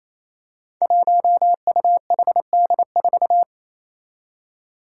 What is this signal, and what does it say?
Morse code, heard as a keyed tone, about 27 wpm, 700 Hz. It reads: 1UHD4
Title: Five Challenging Letter-Number Combinations 28wpm